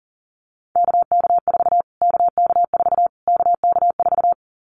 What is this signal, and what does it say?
Morse code, heard as a keyed tone, about 37 wpm, 700 Hz. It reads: KK4 KK4 KK4
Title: Callsign Prefixes for Top 25 DXCC - Repeated 3x - 40wpm